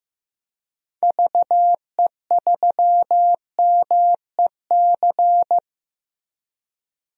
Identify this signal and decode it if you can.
Morse code, heard as a keyed tone, about 15 wpm, 700 Hz. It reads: VE3MEC
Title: DXCC-100 Call Signs - 6 Characters - Repeated 3x - 15wpm